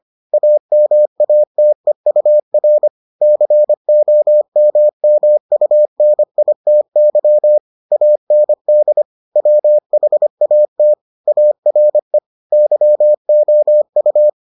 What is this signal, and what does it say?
Morse code, heard as a keyed tone, about 25 wpm, 600 Hz. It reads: AMATEUR COMMUNITY AND WHAT ARE YOU